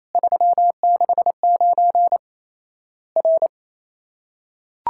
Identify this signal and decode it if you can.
Morse code, heard as a keyed tone, about 28 wpm, 700 Hz. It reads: 369 R
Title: Worked All Europe DX Contest - QTC Exchange - 28wpm